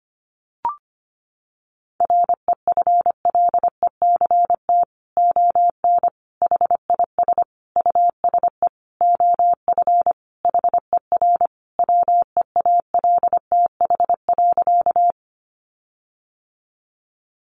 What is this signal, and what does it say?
Morse code, heard as a keyed tone, about 25 wpm, 700 Hz. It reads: REFLECT ON HIS USE OF HER WEALTH.
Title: Sentences from Top 1500 Words 25wpm